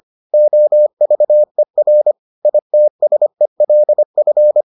Morse code, heard as a keyed tone, about 25 wpm, 600 Hz. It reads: OVER ITSELF